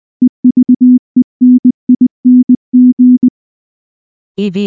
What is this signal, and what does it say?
Morse code, heard as a keyed tone, about 20 wpm, 250 Hz. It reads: EVENING
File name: morse / morse-training-20WPM-9